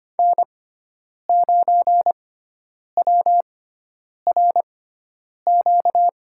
Morse code, heard as a keyed tone, about 25 wpm, spaced as for 9 wpm, 700 Hz. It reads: N9WRQ
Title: USA Call Signs - 5 Characters - 6x Character Spacing - 25wpm